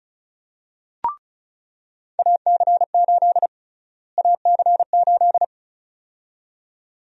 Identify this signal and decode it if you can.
Morse code, heard as a keyed tone, about 36 wpm, 700 Hz. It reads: AC8 AC8
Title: USA Call Sign Prefixes - Repeated 2x - 35wpm